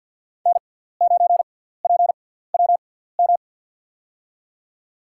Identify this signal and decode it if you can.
Morse code, heard as a keyed tone, about 48 wpm, spaced as for 17 wpm, 700 Hz. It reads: N9PWM
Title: USA Call Signs - 6x Character Spacing - 50wpm